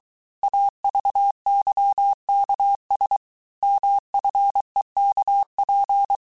Morse code, heard as a keyed tone, about 23 wpm, 800 Hz. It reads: AVYKS MFEKP
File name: cours-du-mercredi / audios / cours-855-texte-chiffre-vitesse-22-mpm